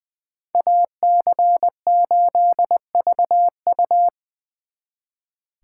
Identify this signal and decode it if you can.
Morse code, heard as a keyed tone, about 23 wpm, 700 Hz. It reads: AC8VU
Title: Speed Racing - USA 2x2 Call Signs 20wpm